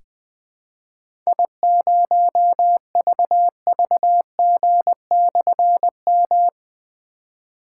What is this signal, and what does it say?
Morse code, heard as a keyed tone, about 20 wpm, 700 Hz. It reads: I0VVG/M
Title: Difficult Call Signs 20wpm